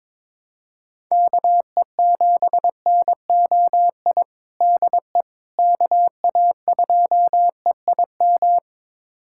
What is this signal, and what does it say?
Morse code, heard as a keyed tone, about 22 wpm, 700 Hz. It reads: KE7NOI DE KA2EIM
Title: USA Callsign DE Callsign - 6 Characters - 22wpm